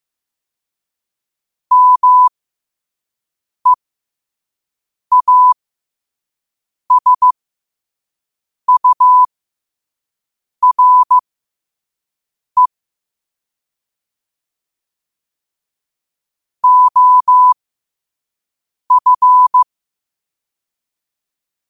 Morse code, heard as a keyed tone, about 15 wpm, spaced as for 5 wpm, 1000 Hz. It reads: MEASURE OF
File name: qotdfiles / 5wpm